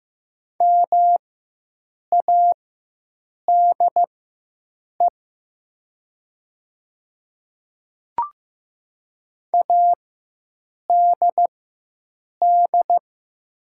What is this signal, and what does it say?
Morse code, heard as a keyed tone, about 15 wpm, spaced as for 7 wpm, 700 Hz. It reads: MADE ADD
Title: 4x Character Spacing - Top 500 Words - 15wpm